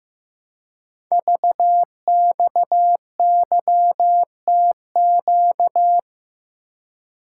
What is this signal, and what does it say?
Morse code, heard as a keyed tone, about 15 wpm, 700 Hz. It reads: VXYTQ
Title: Code Groups - 5 Letter:Numbers - 15wpm